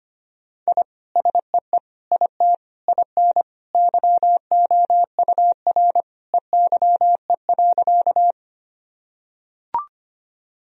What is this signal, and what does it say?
Morse code, heard as a keyed tone, about 25 wpm, 700 Hz. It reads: I SEE IT IN YOUR EYE.
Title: Sentences from Top 1000 Words - Review Part 1 - 25wpm